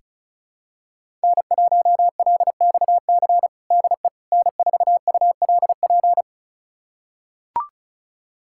Morse code, heard as a keyed tone, about 34 wpm, 700 Hz. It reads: N1LXC DE N4ULP
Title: USA Callsign DE Callsign - 5 Characters - 35wpm